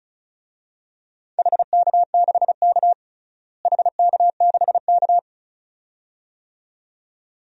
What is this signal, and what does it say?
Morse code, heard as a keyed tone, about 36 wpm, 700 Hz. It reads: HK6K HK6K
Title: DXCC-100 Call Signs - 4 Characters - Repeated 2x - 35wpm